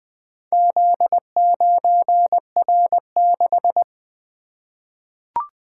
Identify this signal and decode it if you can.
Morse code, heard as a keyed tone, about 20 wpm, 700 Hz. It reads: Z9R6
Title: Four Letter-Number Combinations - 20wpm